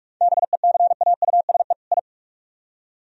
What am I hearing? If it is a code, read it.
Morse code, heard as a keyed tone, about 41 wpm, 700 Hz. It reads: BECAUSE I